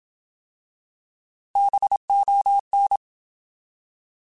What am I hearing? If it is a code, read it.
Morse code, heard as a keyed tone, about 27 wpm, 800 Hz. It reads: BON
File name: Alphabetical / Alphabetical mp3s / Bs @25